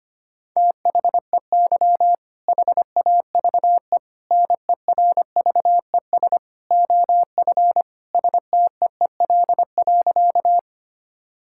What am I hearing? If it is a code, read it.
Morse code, heard as a keyed tone, about 25 wpm, 700 Hz. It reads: THEY HAVE NERVES OF STEEL.
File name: practice / 25wpm / All-Age Appropriate Jokes - 25wpm